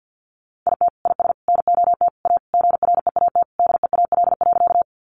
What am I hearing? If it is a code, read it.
Morse code, heard as a keyed tone, about 50 wpm, 700 Hz. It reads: IT IS NOT A GREAT DEAL.
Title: Sentences from Top 800 Words 50wpm